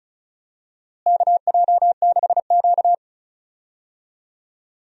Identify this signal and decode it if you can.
Morse code, heard as a keyed tone, about 36 wpm, 700 Hz. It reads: KJ6Q
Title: USA Call Signs - 4 Characters - Repeated 3x - 35wpm